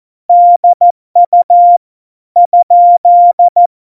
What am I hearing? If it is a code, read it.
Morse code, heard as a keyed tone, about 14 wpm, 700 Hz. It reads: DU ?
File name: Morsekurs / Witze / Witz 03 April 2018, 70 F80